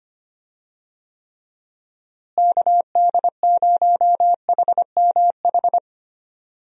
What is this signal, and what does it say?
Morse code, heard as a keyed tone, about 25 wpm, 700 Hz. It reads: KD0HMH
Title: USA Call Signs - 6 Characters - Repeated 2x - 25wpm